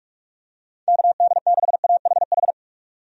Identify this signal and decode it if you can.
Morse code, heard as a keyed tone, about 47 wpm, 700 Hz. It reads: KD6AHH
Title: USA Call Signs - 6 Characters - 45wpm